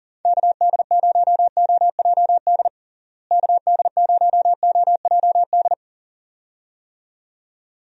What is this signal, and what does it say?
Morse code, heard as a keyed tone, about 40 wpm, 700 Hz. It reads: KD0OJD KD0OJD
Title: USA Call Signs - 6 Characters - Repeated 2x - 40wpm